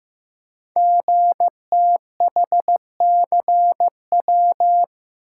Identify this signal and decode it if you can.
Morse code, heard as a keyed tone, about 15 wpm, 700 Hz. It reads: GTHCW
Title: Code Groups - 5 Letters - 15wpm